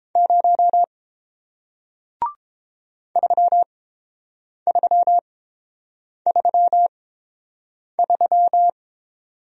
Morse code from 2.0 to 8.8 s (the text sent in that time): 3333